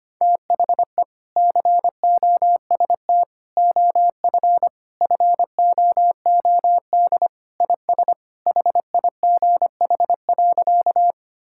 Morse code, heard as a keyed tone, about 25 wpm, 700 Hz. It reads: THE COST OF FOOD IS HIGH.
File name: practice / 25wpm / Sentences with 3-Word Phrases - 25wpm